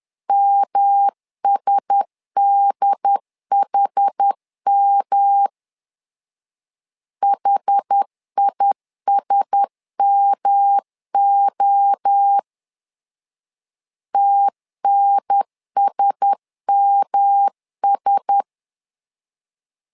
Morse code, heard as a keyed tone, about 10 wpm, 800 Hz. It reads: MSDHM HISMO TNSMS
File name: LEZIONI / 4 Q1S 10wpm 2133